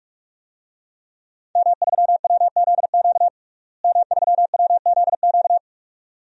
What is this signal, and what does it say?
Morse code, heard as a keyed tone, about 46 wpm, 700 Hz. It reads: M3WZQ M3WZQ
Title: DXCC-100 Call Signs - 5 Characters - Repeated 2x - 45wpm